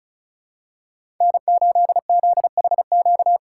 Morse code, heard as a keyed tone, about 34 wpm, 700 Hz. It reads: N8ZHQ
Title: USA Call Signs - 5 Characters - 2x Character Spacing - 35wpm